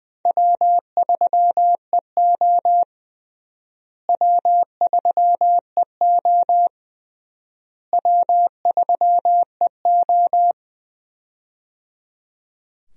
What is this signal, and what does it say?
Morse code, heard as a keyed tone, about 21 wpm, 700 Hz. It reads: W3EO W3EO W3EO
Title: USA Call Signs - 4 Characters - Repeated 3x - 20wpm